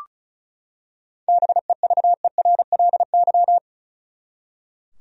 Morse code, heard as a keyed tone, about 36 wpm, 700 Hz. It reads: BEVERLY
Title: USA Common Names - 35wpm